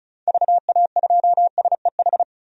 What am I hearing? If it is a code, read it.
Morse code, heard as a keyed tone, about 36 wpm, 700 Hz. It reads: VA2SEH
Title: DXCC-100 Call Signs - 6 Characters - 35wpm